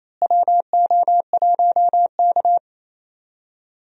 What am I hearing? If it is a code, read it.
Morse code, heard as a keyed tone, about 28 wpm, 700 Hz. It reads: WO1K
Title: USA Call Signs - 4 Characters - Repeated 2x - 28wpm